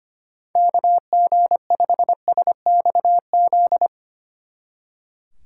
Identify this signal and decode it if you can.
Morse code, heard as a keyed tone, about 32 wpm, 700 Hz. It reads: KG5SXZ